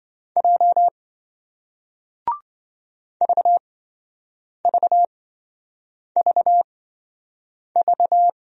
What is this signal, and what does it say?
Morse code, heard as a keyed tone, about 27 wpm, spaced as for 7 wpm, 700 Hz. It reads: J VVVV